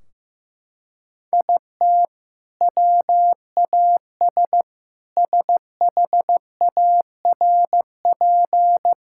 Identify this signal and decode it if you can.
Morse code, heard as a keyed tone, about 15 wpm, 700 Hz. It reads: IT WAS SHARP
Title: Sentences from Top 1600 Words 15wpm